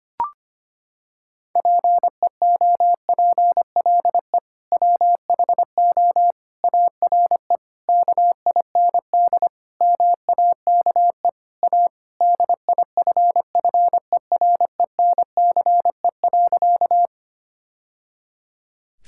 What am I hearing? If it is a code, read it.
Morse code, heard as a keyed tone, about 25 wpm, 700 Hz. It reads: PEOPLE WHO ARE KIND MAKE A DIFFERENCE.
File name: practice / 25wpm / Sentences with 3-Word Phrases - 25wpm